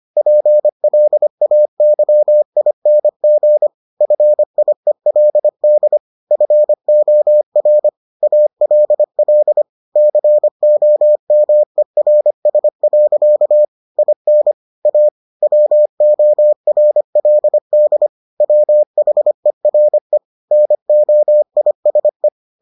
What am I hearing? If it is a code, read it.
Morse code, heard as a keyed tone, about 25 wpm, 600 Hz. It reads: PLAYING FIELD FOR ALL COMERS. IN A WORLD WHERE NOISE